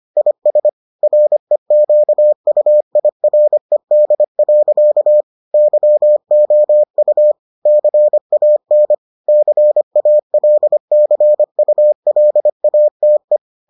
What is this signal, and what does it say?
Morse code, heard as a keyed tone, about 25 wpm, 600 Hz. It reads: IS REQUIRED. YOU CAN CALCULATE